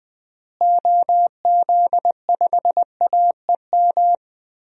Morse code, heard as a keyed tone, about 19 wpm, 700 Hz. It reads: OZ5AEM
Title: DXCC-100 Call Signs - 6 Characters - Repeated 3x - 20wpm